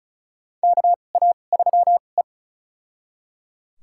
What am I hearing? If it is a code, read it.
Morse code, heard as a keyed tone, about 34 wpm, spaced as for 25 wpm, 700 Hz. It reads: KA3E
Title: USA Call Signs - 4 Characters - 2x Character Spacing - 35wpm